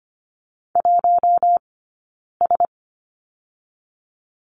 Morse code, heard as a keyed tone, about 25 wpm, 700 Hz. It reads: 1 S
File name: course / 25:10wpm / 092 Lesson 16d - Letter c - Two Characters 25wpm